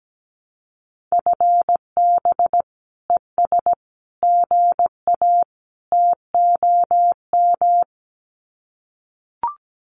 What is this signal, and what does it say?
Morse code, heard as a keyed tone, about 17 wpm, 700 Hz. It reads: FB ES GA TOM
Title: Common QSO Phrases Encore 17wpm